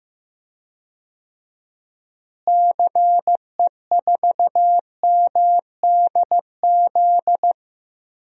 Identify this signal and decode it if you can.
Morse code, heard as a keyed tone, about 15 wpm, 700 Hz. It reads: CE4MDZ
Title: DXCC-100 Call Signs - 6 Characters - Repeated 3x - 15wpm